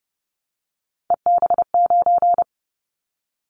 Morse code, heard as a keyed tone, about 30 wpm, 700 Hz. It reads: EB9
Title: Callsign Prefixes for Top 100 DXCC - 30wpm